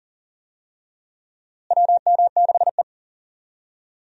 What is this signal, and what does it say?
Morse code, heard as a keyed tone, about 40 wpm, 700 Hz. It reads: WM6E
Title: USA Call Signs - 4 Characters - Repeated 2x - 40wpm